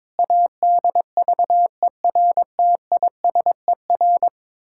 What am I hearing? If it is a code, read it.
Morse code, heard as a keyed tone, about 22 wpm, 700 Hz. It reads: ADVERTISER
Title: Pangram Challenge v2 - 22wpm